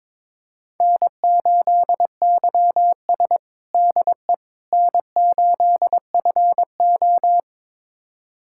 Morse code, heard as a keyed tone, about 22 wpm, 700 Hz. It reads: N8YS DE N8FO
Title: USA Callsign DE Callsign - 4 Characters - 22wpm